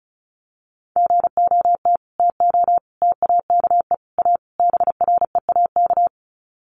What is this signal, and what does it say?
Morse code, heard as a keyed tone, about 34 wpm, 700 Hz. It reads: GOT TO TAKE A BREAK
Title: Sets of 5 Words Encore 35wpm